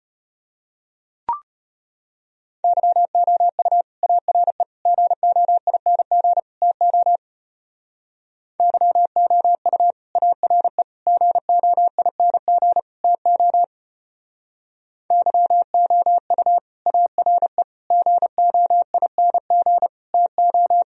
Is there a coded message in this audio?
Yes: Morse code from 2.6 to 20.8 s, heard as a keyed tone, about 31 wpm, 700 Hz. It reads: YOU ARE GOING TO YOU ARE GOING TO YOU ARE GOING TO